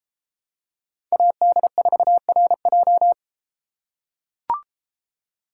Morse code, heard as a keyed tone, about 32 wpm, 700 Hz. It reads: AD4RJ